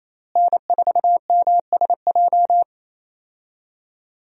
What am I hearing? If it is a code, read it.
Morse code, heard as a keyed tone, about 28 wpm, 700 Hz. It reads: N4MSJ
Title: USA Call Signs - 5 Characters - Repeated 2x - 28wpm